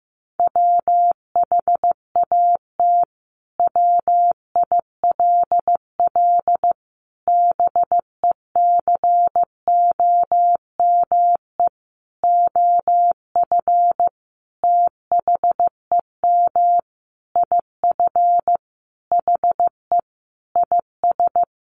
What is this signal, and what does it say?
Morse code, heard as a keyed tone, about 14 wpm, 700 Hz. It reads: WHAT WILL BECOME OF THEM IF HE IS
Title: Sentences from Top 1100 Words 15wpm